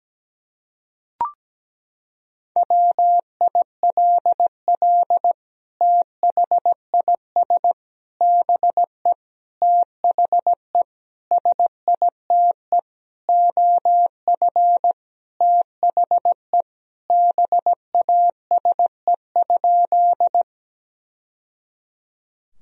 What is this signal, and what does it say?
Morse code, heard as a keyed tone, about 17 wpm, 700 Hz. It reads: WILL THIS BE THE SITE OF THE BASE?
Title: Sentences from Top 1000 Words - Review Part 1 - 17wpm